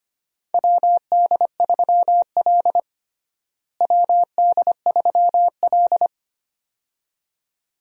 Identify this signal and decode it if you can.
Morse code, heard as a keyed tone, about 25 wpm, 700 Hz. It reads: WD3L WD3L
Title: USA Call Signs - 4 Characters - Repeated 2x - 25wpm